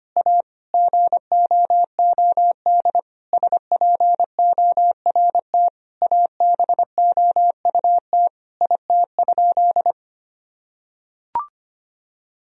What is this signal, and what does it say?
Morse code, heard as a keyed tone, about 25 wpm, 700 Hz. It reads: A GOOD SPORT ABOUT IT?
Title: Sentences from Top 1000 Words - Review Part 4 - 25wpm